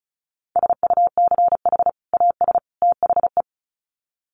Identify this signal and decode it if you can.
Morse code, heard as a keyed tone, about 34 wpm, 700 Hz. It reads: SUCH AS THE
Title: Sets of 3 Words 35wpm